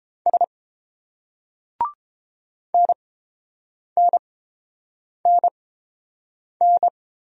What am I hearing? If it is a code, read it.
Morse code, heard as a keyed tone, about 28 wpm, spaced as for 7 wpm, 690 Hz. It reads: S NNNN